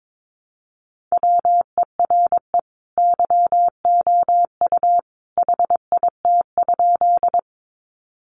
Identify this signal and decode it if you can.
Morse code, heard as a keyed tone, about 22 wpm, 700 Hz. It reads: WERE YOU HIT?
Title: Sentences from Top 1200 Words 22wpm